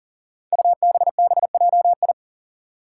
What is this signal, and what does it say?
Morse code, heard as a keyed tone, about 40 wpm, 700 Hz. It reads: UBBJI
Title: Five Challenging Letter-Number Combinations 40wpm